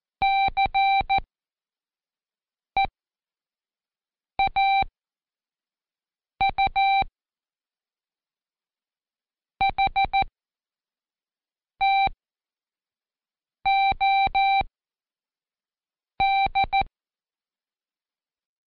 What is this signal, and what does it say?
Morse code, heard as a keyed tone, about 14 wpm, spaced as for 5 wpm, 780 Hz. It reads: CEAU HTOD
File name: Code Quick / AudioDownld1 / Seg37-Day4-Quiz CHLN